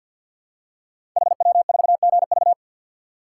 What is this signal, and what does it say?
Morse code, heard as a keyed tone, about 49 wpm, 700 Hz. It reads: HW4GV
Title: Code Groups - 5 Letter:Numbers - 50wpm